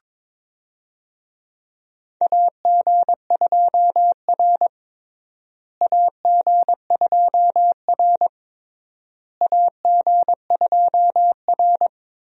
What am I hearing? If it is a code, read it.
Morse code, heard as a keyed tone, about 21 wpm, 700 Hz. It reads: AG2R AG2R AG2R
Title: USA Call Signs - 4 Characters - Repeated 3x - 22wpm